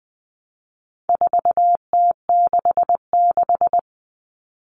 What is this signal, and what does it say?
Morse code, heard as a keyed tone, about 20 wpm, 700 Hz. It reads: 4T66
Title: All Cut Numbers - 4-Digit 20wpm